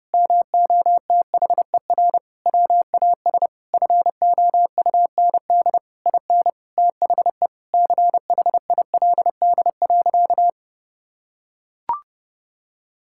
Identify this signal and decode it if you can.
Morse code, heard as a keyed tone, about 30 wpm, 700 Hz. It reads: MOTHER WAS FOUND IN THE CHILD.
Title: Sentences from Top 1000 Words - Review Part 5 - 30wpm